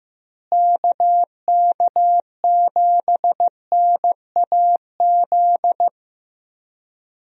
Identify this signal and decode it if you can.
Morse code, heard as a keyed tone, about 15 wpm, 700 Hz. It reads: KK7NAZ
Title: USA Call Signs - 6 Characters - Repeated 3x - 15wpm